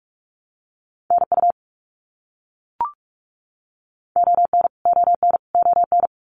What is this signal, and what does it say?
Morse code, heard as a keyed tone, about 44 wpm, 700 Hz. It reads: NU ON ON ON
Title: Canadian Provinces and Territories - Repeated 3x - 45wpm